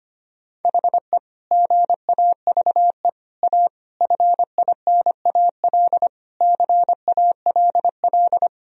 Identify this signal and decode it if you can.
Morse code, heard as a keyed tone, about 25 wpm, 700 Hz. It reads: HE GAVE A FINAL CALL
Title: Sentences from Top 1000 Words - Review Part 2 - 25wpm